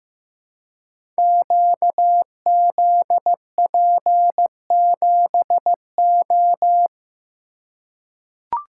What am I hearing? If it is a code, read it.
Morse code, heard as a keyed tone, about 15 wpm, 700 Hz. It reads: QZP7O
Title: Code Groups - 5 Letter:Numbers - 15wpm